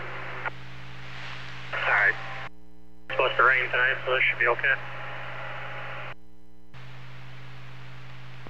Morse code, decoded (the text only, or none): none